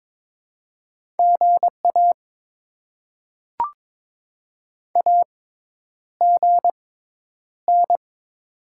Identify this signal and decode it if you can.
Morse code, heard as a keyed tone, about 22 wpm, spaced as for 10 wpm, 690 Hz. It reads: GA AGN
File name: Top 20 QSO Elements - 6x Character Spacing - 22wpm